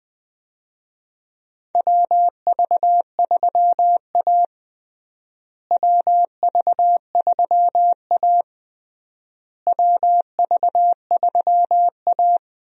Morse code, heard as a keyed tone, about 20 wpm, 700 Hz. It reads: WV3A WV3A WV3A
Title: USA Call Signs - 4 Characters - Repeated 3x - 20wpm